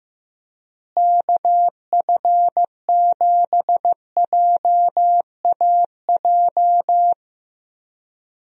Morse code, heard as a keyed tone, about 15 wpm, 700 Hz. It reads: KF7JAJ
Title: USA Call Signs - 6 Characters - Repeated 2x - 15wpm